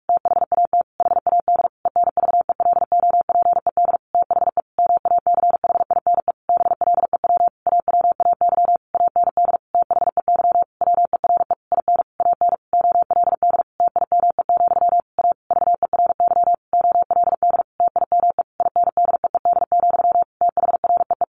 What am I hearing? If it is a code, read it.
Morse code, heard as a keyed tone, about 46 wpm, 700 Hz. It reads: THAT HAD ENVELOPED THE MACHINE BLEW AWAY AND THEY WERE IN AN OLD TIME, A VERY OLD TIME INDEED, THREE
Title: A Sound Of Thunder 45wpm